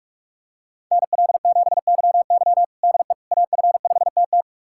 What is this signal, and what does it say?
Morse code, heard as a keyed tone, about 44 wpm, 700 Hz. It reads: NL7YY DE AF5TT
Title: USA Callsign DE Callsign - 5 Characters - 45wpm